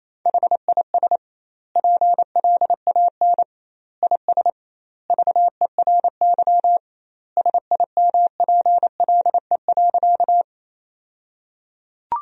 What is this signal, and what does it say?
Morse code, heard as a keyed tone, about 28 wpm, 700 Hz. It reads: HIS PLAN IS VERY SIMPLE.
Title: 2x Word-Spacing - Sentences from Top 500 Words - 28wpm